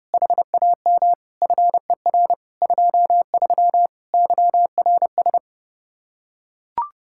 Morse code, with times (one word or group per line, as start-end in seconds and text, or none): 0.1-1.1 HAM
1.4-2.3 FER
2.6-3.9 23
4.1-5.4 YRS